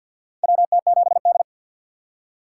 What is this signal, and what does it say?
Morse code, heard as a keyed tone, about 51 wpm, 700 Hz. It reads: WT7D
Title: USA Call Signs - 4 Characters - 3x Character Spacing - 50wpm